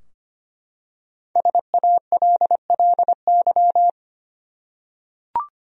Morse code, heard as a keyed tone, about 25 wpm, 700 Hz. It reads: SALLY
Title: USA Common Names - Max 5 Letters - 25wpm